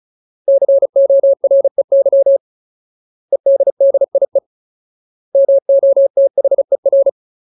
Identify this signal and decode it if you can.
Morse code, heard as a keyed tone, about 35 wpm, 550 Hz. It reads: COREY EDDIE MOTHER
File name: Ultimate Word List / 35WPM Ultimate